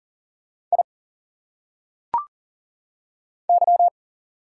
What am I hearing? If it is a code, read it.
Morse code, heard as a keyed tone, about 40 wpm, 700 Hz. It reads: I Y
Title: Single Letters - Phonetic Pronunciation 40wpm